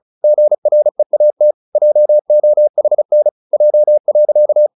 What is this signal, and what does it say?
Morse code, heard as a keyed tone, about 35 wpm, 600 Hz. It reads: GREAT JOHN J.
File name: the old man and the sea / the old man and the sea 1 35wpm